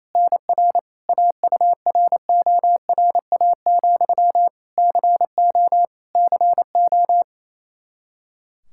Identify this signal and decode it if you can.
Morse code, heard as a keyed tone, about 28 wpm, 700 Hz. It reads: NR AURORA, CO CO
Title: Practice QSO - Exchange 1 - Beginning - 28wpm